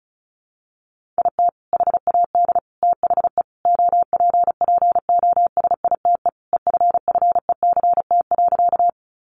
Morse code, heard as a keyed tone, about 34 wpm, 700 Hz. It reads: IT HAD THE OPPOSITE EFFECT.
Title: Sentences from Top 1300 Words 35wpm